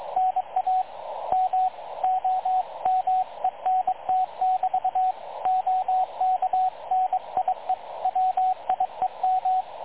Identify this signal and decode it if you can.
Morse code, heard as a keyed tone, about 23 wpm, 740 Hz, with radio noise. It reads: NA MOMENT<BT> OK NIE WIEM
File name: dlazaawansowanych / nagraniazpasma / websdr recording 3567.0kHz